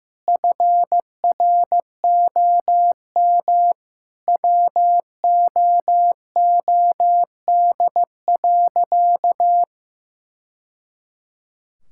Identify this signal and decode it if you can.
Morse code, heard as a keyed tone, about 14 wpm, 700 Hz. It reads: FROM WOOD.
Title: Sentences from Top 1000 Words - Review Part 6 - 15wpm